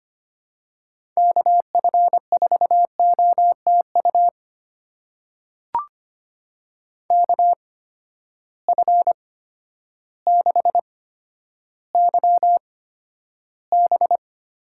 Morse code, heard as a keyed tone, about 25 wpm, spaced as for 11 wpm, 700 Hz. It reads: KF4OTU KF6YB
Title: USA Call Signs - 6 Characters - 8x Character Spacing - 25wpm